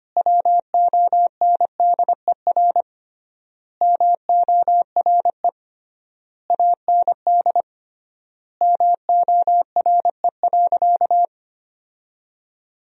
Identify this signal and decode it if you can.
Morse code, heard as a keyed tone, about 25 wpm, 700 Hz. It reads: WONDER MORE AND MORE.
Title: 3x Word-Spacing - Sentences from Top 500 Words - 25wpm